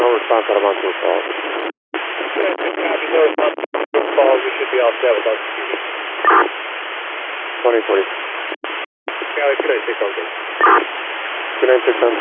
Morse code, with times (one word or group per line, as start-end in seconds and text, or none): none